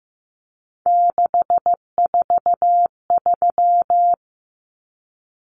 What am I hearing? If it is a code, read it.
Morse code, heard as a keyed tone, about 15 wpm, 700 Hz. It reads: 643
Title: Numbers - 3-Digit 15wpm